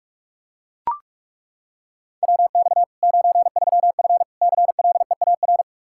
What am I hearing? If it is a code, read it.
Morse code, heard as a keyed tone, about 41 wpm, 700 Hz. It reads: WX 93F CLEAR
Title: Practice QSO - Exchange 2 - Info - 45wpm